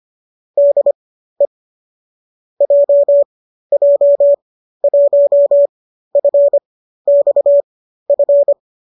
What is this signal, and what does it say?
Morse code, heard as a keyed tone, about 25 wpm, spaced as for 13 wpm, 570 Hz. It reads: DE JJ1FXF